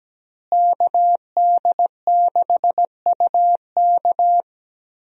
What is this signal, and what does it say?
Morse code, heard as a keyed tone, about 17 wpm, 700 Hz. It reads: KD6UK
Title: USA Call Signs - 5 Characters - Repeated 2x - 17wpm